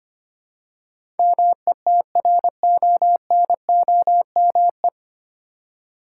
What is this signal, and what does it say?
Morse code, heard as a keyed tone, about 25 wpm, 700 Hz. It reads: METRONOME
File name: practice / 25wpm / Difficult Words with Hints - 25wpm